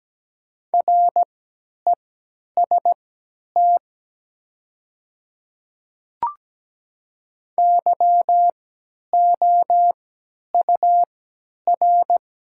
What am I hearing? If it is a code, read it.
Morse code, heard as a keyed tone, about 17 wpm, spaced as for 10 wpm, 700 Hz. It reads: REST YOUR
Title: 3x Character Spacing - Top 500 Words - 17wpm